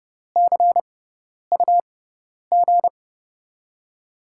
Morse code, heard as a keyed tone, about 30 wpm, spaced as for 10 wpm, 700 Hz. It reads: CUG
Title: USA Call Signs - 5 Characters - 6x Character Spacing - 30wpm